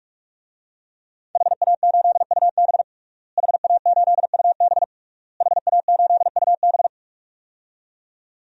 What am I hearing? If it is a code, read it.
Morse code, heard as a keyed tone, about 47 wpm, 700 Hz. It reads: HA8UB HA8UB HA8UB
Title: DXCC-100 Call Signs - 5 Characters - Repeated 3x - 45wpm